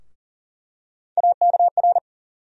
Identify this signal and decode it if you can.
Morse code, heard as a keyed tone, about 40 wpm, 700 Hz. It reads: AKR
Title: USA Call Sign Postfixes - 3 Letters 40wpm